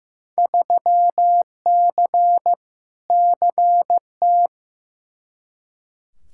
Morse code, heard as a keyed tone, about 15 wpm, 700 Hz. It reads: 3C CT
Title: Field Day Exchanges Encore - 15wpm